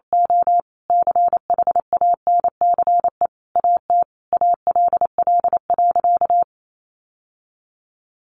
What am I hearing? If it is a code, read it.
Morse code, heard as a keyed tone, about 28 wpm, 700 Hz. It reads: O CHANCE AT ALL.
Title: Sentences from Top 800 Words 28wpm